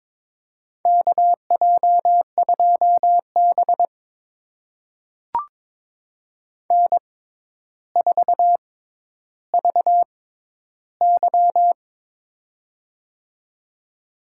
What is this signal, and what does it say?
Morse code, heard as a keyed tone, about 22 wpm, spaced as for 11 wpm, 700 Hz. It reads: KJ2B N4VY
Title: USA Call Signs - 4 Characters - 6x Character Spacing - 22wpm